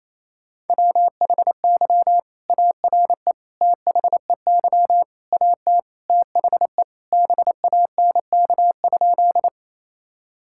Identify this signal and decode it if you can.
Morse code, heard as a keyed tone, about 28 wpm, 700 Hz. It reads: WHY ARE THEY AT THE BANK?
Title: Sentences from Top 1000 Words - Review Part 6 - 28wpm